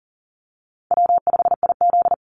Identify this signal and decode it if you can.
Morse code, heard as a keyed tone, about 40 wpm, 700 Hz. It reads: W5IZ
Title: Easy Call Signs - 40wpm